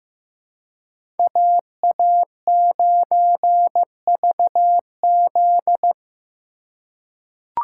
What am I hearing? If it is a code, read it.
Morse code, heard as a keyed tone, about 15 wpm, 700 Hz. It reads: AA9VZ
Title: USA Call Signs - 3x Character Spacing - 15wpm